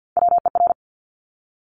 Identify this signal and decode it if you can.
Morse code, heard as a keyed tone, about 50 wpm, 700 Hz. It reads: WER
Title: Top Three Letter Combinations in Words 50wpm